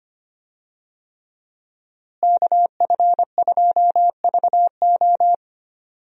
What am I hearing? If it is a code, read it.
Morse code, heard as a keyed tone, about 25 wpm, 700 Hz. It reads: KF2VO
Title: USA Call Signs - 5 Characters - Repeated 3x - 25wpm